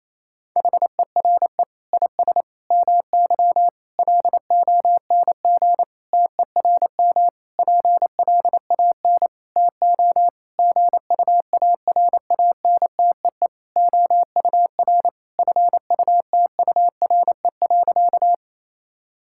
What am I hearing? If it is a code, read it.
Morse code, heard as a keyed tone, about 28 wpm, 700 Hz. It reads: HERE IS MY LONG TERM PLAN TO GUARANTEE OUR FUTURE.
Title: Sentences from Top 1600 Words 28wpm